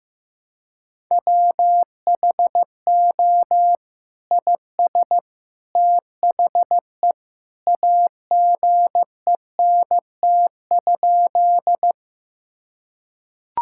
Morse code, heard as a keyed tone, about 15 wpm, 700 Hz. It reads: WHO IS THE AGENT?
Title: Sentences from Top 1600 Words 15wpm